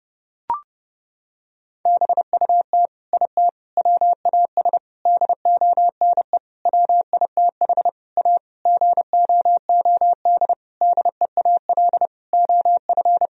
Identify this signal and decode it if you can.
Morse code, heard as a keyed tone, about 32 wpm, 700 Hz. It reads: BUT IT WAS DONE WITH A GOOD DEAL OF
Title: Sentences from Top 1000 Words - Review Part 4 - 30wpm